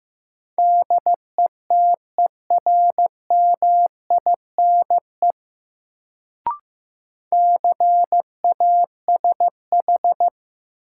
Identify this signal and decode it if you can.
Morse code, heard as a keyed tone, about 15 wpm, 700 Hz. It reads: DETERMINE CASH
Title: Sentences from Top 700 Words - New Word Drill - 15wpm